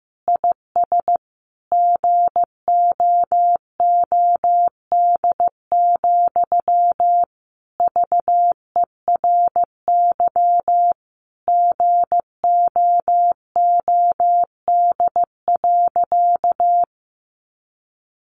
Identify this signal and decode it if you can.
Morse code, heard as a keyed tone, about 15 wpm, 700 Hz. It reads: IS GOOD, VERY GOOD.
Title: Sentences from Top 200 Words 15wpm